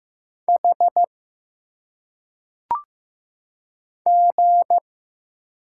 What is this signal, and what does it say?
Morse code, heard as a keyed tone, about 15 wpm, 700 Hz. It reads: H G
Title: Single Letters - Phonetic Pronunciation 15wpm